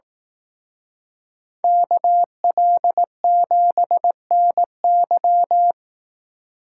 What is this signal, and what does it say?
Morse code, heard as a keyed tone, about 18 wpm, 700 Hz. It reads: KL7NY
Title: Speed Racing - USA Call Signs 15wpm